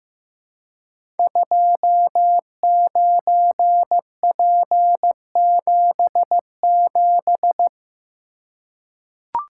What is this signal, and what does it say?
Morse code, heard as a keyed tone, about 15 wpm, 700 Hz. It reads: 29P77
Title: Code Groups - 5 Letter:Numbers - 15wpm